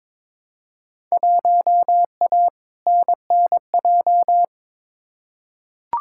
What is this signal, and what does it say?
Morse code, heard as a keyed tone, about 22 wpm, 700 Hz. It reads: 1A NNJ
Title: Field Day Exchanges Encore - 22wpm